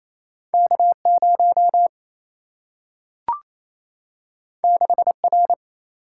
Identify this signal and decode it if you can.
Morse code, heard as a keyed tone, about 28 wpm, 700 Hz. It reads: K 0 6 R
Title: Single-Letter Single-Number 28wpm